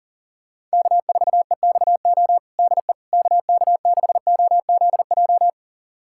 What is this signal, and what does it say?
Morse code, heard as a keyed tone, about 40 wpm, 700 Hz. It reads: K4EXO DE KK6OZJ
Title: USA Callsign DE Callsign 40wpm